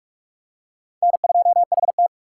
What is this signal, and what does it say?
Morse code, heard as a keyed tone, about 46 wpm, 690 Hz. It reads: N2HT
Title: USA Call Signs - 4 Characters - 3x Character Spacing - 45wpm